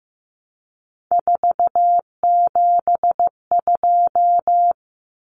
Morse code, heard as a keyed tone, about 15 wpm, 700 Hz. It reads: 472
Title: Numbers - 3-Digit 15wpm